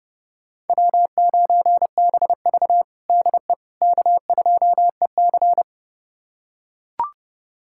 Morse code, heard as a keyed tone, about 30 wpm, 700 Hz. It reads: W9BV DE K2EC
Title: USA Callsign DE Callsign - 4 Characters - 30wpm